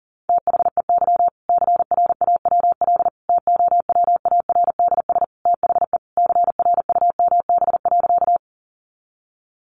Morse code, heard as a keyed tone, about 40 wpm, 700 Hz. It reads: THEY CRAWL TOWARDS THE CRUMB.
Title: Easy Phrases 40wpm